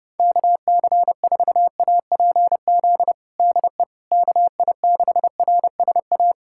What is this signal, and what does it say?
Morse code, heard as a keyed tone, about 30 wpm, 700 Hz. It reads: KC4APZ DE KI6RSA
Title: USA Callsign DE Callsign - 6 Characters - 30wpm